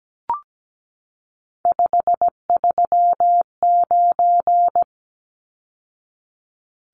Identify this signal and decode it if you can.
Morse code, heard as a keyed tone, about 17 wpm, 700 Hz. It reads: 539
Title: Numbers - 3-Digit 17wpm